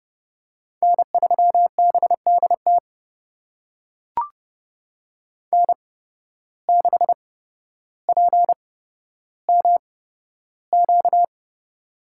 Morse code, heard as a keyed tone, about 30 wpm, spaced as for 13 wpm, 700 Hz. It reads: N3BDT N6PMQ
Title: USA Call Signs - 5 Characters - 8x Character Spacing - 30wpm